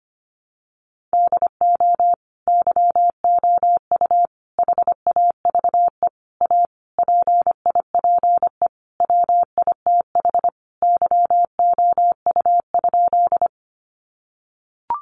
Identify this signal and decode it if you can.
Morse code, heard as a keyed tone, about 25 wpm, 700 Hz. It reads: DO YOU HAVE A PIPE WITH YOU?
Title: Sentences from Top 1500 Words 25wpm